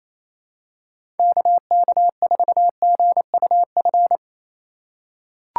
Morse code, heard as a keyed tone, about 28 wpm, 690 Hz. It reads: KK4GUF
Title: USA Call Signs - 6 Characters - 2x Character Spacing - 28wpm